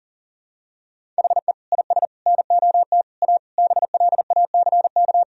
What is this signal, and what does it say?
Morse code, heard as a keyed tone, about 40 wpm, 700 Hz. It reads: HE IS NOT A BLACK
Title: Sentences from Top 1000 Words - Review Part 2 - 40wpm